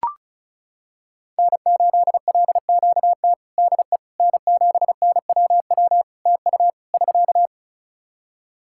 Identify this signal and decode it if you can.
Morse code, heard as a keyed tone, about 36 wpm, 700 Hz. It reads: N8LQT DE N7NWW TU <SK>
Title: Practice QSO - Exchange 4 - Ending - 35wpm